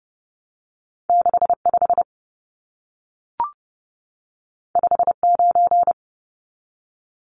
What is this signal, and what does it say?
Morse code, heard as a keyed tone, about 30 wpm, 700 Hz. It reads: 65 59
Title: Numbers - 2-Digit 30wpm